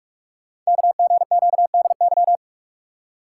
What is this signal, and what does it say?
Morse code, heard as a keyed tone, about 44 wpm, 700 Hz. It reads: KGQDY
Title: Code Groups - 5 Letters - 45wpm